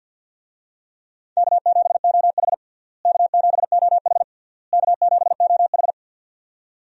Morse code, heard as a keyed tone, about 51 wpm, 700 Hz. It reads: K7OH K7OH K7OH
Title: USA Call Signs - 4 Characters - Repeated 3x - 50wpm